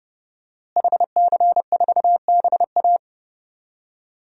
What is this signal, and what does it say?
Morse code, heard as a keyed tone, about 30 wpm, 700 Hz. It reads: HC4BA
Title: DXCC-100 Call Signs - 5 Characters - Repeated 3x - 30wpm